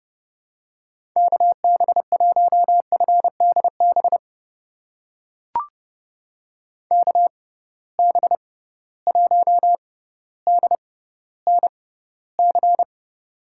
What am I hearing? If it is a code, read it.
Morse code, heard as a keyed tone, about 32 wpm, spaced as for 16 wpm, 700 Hz. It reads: KB1FDB KB1DNC
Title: USA Call Signs - 6 Characters - 6x Character Spacing - 30wpm